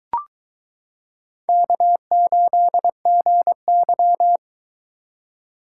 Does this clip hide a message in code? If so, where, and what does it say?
Yes: Morse code from 1.5 to 4.4 s, heard as a keyed tone, about 22 wpm, 700 Hz. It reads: K8GY